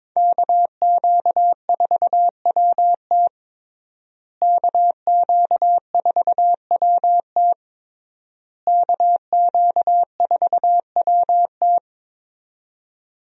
Morse code, heard as a keyed tone, about 22 wpm, 700 Hz. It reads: KQ4WT KQ4WT KQ4WT
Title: USA Call Signs - 5 Characters - Repeated 3x - 22wpm